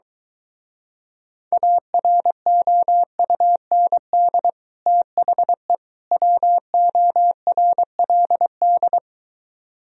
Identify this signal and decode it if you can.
Morse code, heard as a keyed tone, about 23 wpm, 700 Hz. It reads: AROUND THE WORLD